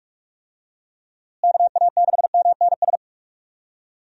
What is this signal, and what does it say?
Morse code, heard as a keyed tone, about 46 wpm, 700 Hz. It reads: KA6MNS
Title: USA Call Signs - 6 Characters - Repeated 3x - 45wpm